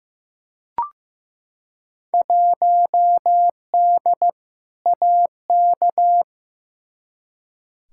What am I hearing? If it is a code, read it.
Morse code, heard as a keyed tone, about 15 wpm, 700 Hz. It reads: 1D AK
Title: Field Day Exchanges - 15wpm